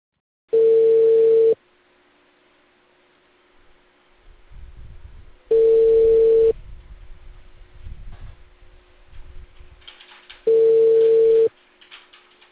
Ringback tone (China, 450 Hz).